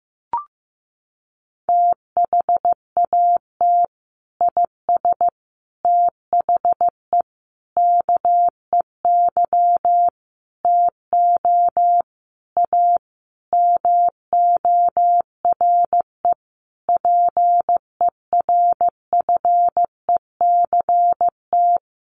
Morse code, heard as a keyed tone, about 15 wpm, 700 Hz. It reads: THAT IS THE KEY TO A MORE PERFECT
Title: Sentences from Top 1400 Words 15wpm